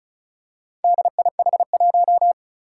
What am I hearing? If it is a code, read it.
Morse code, heard as a keyed tone, about 36 wpm, 700 Hz. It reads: DIH1
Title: Four Challenging Letter-Number Combinations 35wpm